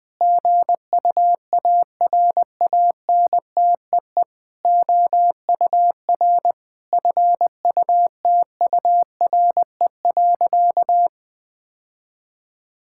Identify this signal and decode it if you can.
Morse code, heard as a keyed tone, about 20 wpm, 700 Hz. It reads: GUARANTEE OUR FUTURE.
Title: Sentences from Top 1600 Words 20wpm